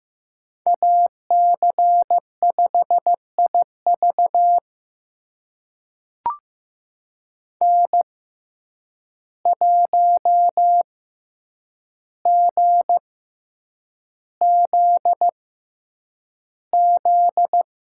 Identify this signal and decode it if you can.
Morse code, heard as a keyed tone, about 15 wpm, spaced as for 8 wpm, 700 Hz. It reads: AC5IV N1GZZ
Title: USA Call Signs - 5 Characters - 6x Character Spacing - 15wpm